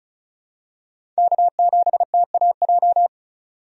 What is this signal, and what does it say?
Morse code, heard as a keyed tone, about 34 wpm, 700 Hz. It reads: K7TAJ
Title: USA Call Signs - 5 Characters - 35wpm